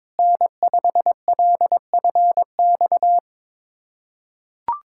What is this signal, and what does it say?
Morse code, heard as a keyed tone, about 22 wpm, 700 Hz. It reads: N5LFX
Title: USA Callsign DE Callsign 22wpm